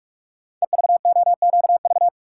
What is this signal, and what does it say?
Morse code, heard as a keyed tone, about 44 wpm, 690 Hz. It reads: EVOQV
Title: Code Groups - 5 Letters - 45wpm